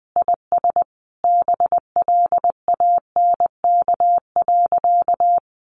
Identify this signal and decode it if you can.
Morse code, heard as a keyed tone, about 20 wpm, 700 Hz. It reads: IS BLANK.
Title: Sentences from Top 1500 Words 20wpm